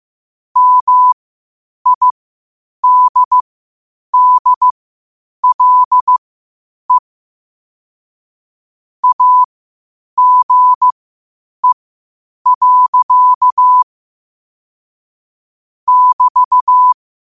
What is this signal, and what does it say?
Morse code, heard as a keyed tone, about 15 wpm, spaced as for 8 wpm, 1000 Hz. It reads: MIDDLE AGE. <BT>
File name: qotdfiles / 8wpm